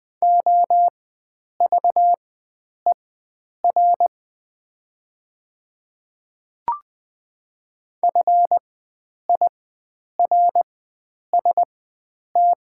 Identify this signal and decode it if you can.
Morse code, heard as a keyed tone, about 20 wpm, spaced as for 9 wpm, 700 Hz. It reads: OVER FIRST
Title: 4x Character Spacing - Top 100 Words - 20wpm